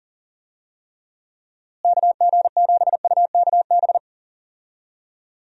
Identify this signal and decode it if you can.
Morse code, heard as a keyed tone, about 40 wpm, 700 Hz. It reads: KG7UKB
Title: USA Call Signs - 6 Characters - Repeated 3x - 40wpm